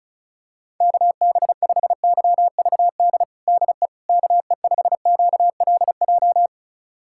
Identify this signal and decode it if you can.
Morse code, heard as a keyed tone, about 34 wpm, 700 Hz. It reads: KB5YVD DE KE5QLJ
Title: USA Callsign DE Callsign - 6 Characters - 35wpm